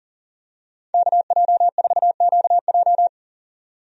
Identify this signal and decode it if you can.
Morse code, heard as a keyed tone, about 40 wpm, 700 Hz. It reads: KJ4QJ
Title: USA Call Signs - 5 Characters - 40wpm